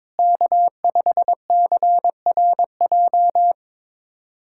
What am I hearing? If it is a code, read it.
Morse code, heard as a keyed tone, about 21 wpm, 700 Hz. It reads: K5CRJ
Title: USA Callsign DE Callsign - 5 Characters - 22wpm